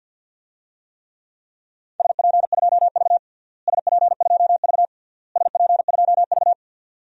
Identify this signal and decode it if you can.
Morse code, heard as a keyed tone, about 48 wpm, 700 Hz. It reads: SP2V SP2V SP2V
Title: DXCC-100 Call Signs - 4 Characters - Repeated 3x - 50wpm